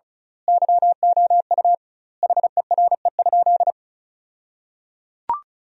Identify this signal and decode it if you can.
Morse code, heard as a keyed tone, about 34 wpm, 700 Hz. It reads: YOU HERE?
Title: 2x Word-Spacing - Sentences from Top 500 Words - 35wpm